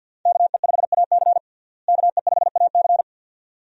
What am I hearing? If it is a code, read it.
Morse code, heard as a keyed tone, about 51 wpm, 700 Hz. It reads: KE5AC KE5AC
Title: USA Call Signs - 5 Characters - Repeated 2x - 50wpm